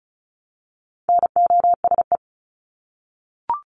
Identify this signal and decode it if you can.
Morse code, heard as a keyed tone, about 34 wpm, 700 Hz. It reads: NOSE
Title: Top 1501-1750 Words 35wpm